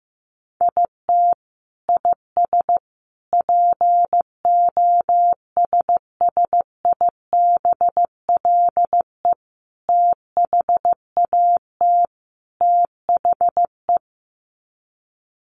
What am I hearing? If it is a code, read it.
Morse code, heard as a keyed tone, about 15 wpm, 700 Hz. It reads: IT IS POSSIBLE THAT THE
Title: Sets of 5 Words Encore2 15wpm